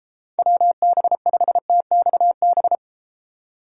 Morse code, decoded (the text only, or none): WB5TXB